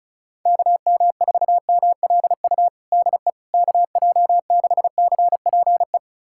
Morse code, heard as a keyed tone, about 36 wpm, 700 Hz. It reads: KM4MLU DE KJ6CPE
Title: USA Callsign DE Callsign 35wpm